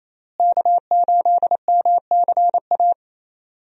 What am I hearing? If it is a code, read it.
Morse code, heard as a keyed tone, about 28 wpm, 700 Hz. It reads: K8MCA
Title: USA Call Signs - 5 Characters - Repeated 2x - 28wpm